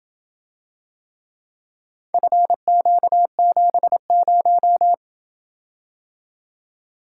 Morse code, heard as a keyed tone, about 28 wpm, 700 Hz. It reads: FQ70